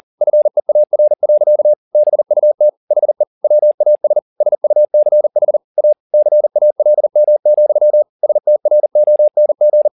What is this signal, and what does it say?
Morse code, heard as a keyed tone, about 40 wpm, 600 Hz. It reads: FEAR. BUT HE WAS SUCH A CALM, STRONG
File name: the old man and the sea 2 40wpm